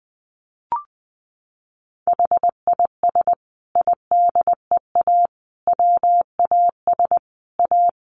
Morse code, heard as a keyed tone, about 20 wpm, 700 Hz. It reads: HIS IDEA WAS A
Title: Sentences from Top 900 Words 20wpm